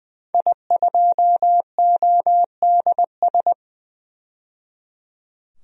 Morse code, heard as a keyed tone, about 20 wpm, 700 Hz. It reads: I2ODS
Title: DXCC-100 Call Signs - 5 Characters - 20wpm